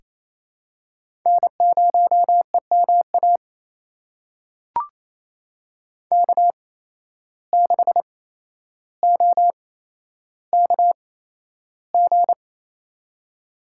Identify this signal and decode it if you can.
Morse code, heard as a keyed tone, about 28 wpm, spaced as for 12 wpm, 700 Hz. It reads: N0EMA K6OKG
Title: USA Call Signs - 5 Characters - 8x Character Spacing - 28wpm